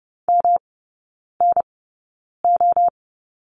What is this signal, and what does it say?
Morse code, heard as a keyed tone, about 30 wpm, spaced as for 9 wpm, 700 Hz. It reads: MNO
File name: Lesson 19e - Letter w - Three Characters 30wpm